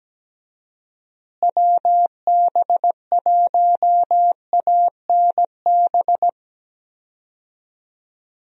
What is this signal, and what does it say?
Morse code, heard as a keyed tone, about 17 wpm, 700 Hz. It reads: WB1ANB
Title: USA Call Signs - 6 Characters - Repeated 3x - 17wpm